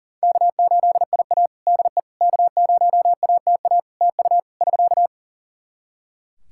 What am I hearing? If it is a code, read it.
Morse code, heard as a keyed tone, about 40 wpm, 700 Hz. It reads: K8IA DE K0ATA TU <SK>
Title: Practice QSO - Exchange 4 - Ending - 40wpm